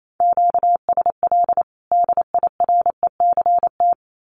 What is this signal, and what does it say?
Morse code, heard as a keyed tone, about 28 wpm, 700 Hz. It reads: QSL DIRECT
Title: Common QSO Phrases 28wpm